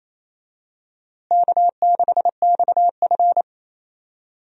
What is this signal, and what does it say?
Morse code, heard as a keyed tone, about 28 wpm, 700 Hz. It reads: K6XF
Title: USA Call Signs - 4 Characters - Repeated 2x - 28wpm